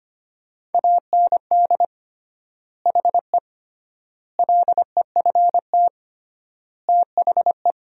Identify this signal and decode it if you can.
Morse code, heard as a keyed tone, about 25 wpm, 700 Hz. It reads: AND HE LEFT THE
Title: 3x Word-Spacing - Sentences from Top 300 Words - 25wpm